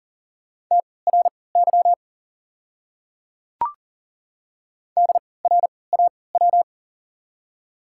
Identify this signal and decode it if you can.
Morse code, heard as a keyed tone, about 39 wpm, spaced as for 22 wpm, 700 Hz. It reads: TRY DRAW
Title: 3x Character Spacing - Top 500 Words - 40wpm